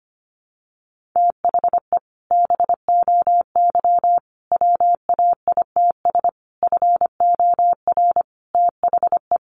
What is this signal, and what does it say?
Morse code, heard as a keyed tone, about 25 wpm, 700 Hz. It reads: THE BOY WAITS FOR THE
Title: Additional Easy Phrases 25wpm